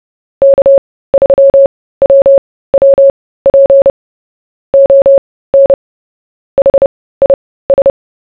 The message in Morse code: K3WWP ON HIS